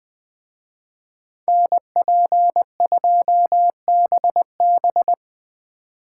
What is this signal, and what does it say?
Morse code, heard as a keyed tone, about 25 wpm, 700 Hz. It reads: NP2BB
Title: Speed Racing - USA 2x2 Call Signs 20wpm